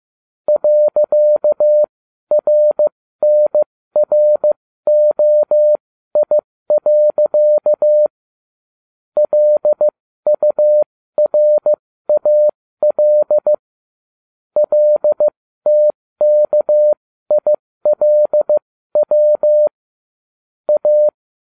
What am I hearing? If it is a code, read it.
Morse code, heard as a keyed tone, about 15 wpm, spaced as for 13 wpm, 600 Hz. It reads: .RNROI. LURAL LTKILW A